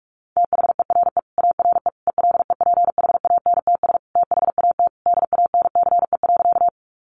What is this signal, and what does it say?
Morse code, heard as a keyed tone, about 44 wpm, 700 Hz. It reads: THERE ARE ELEPHANTS THAT DANCE.
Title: Easy Phrases 45wpm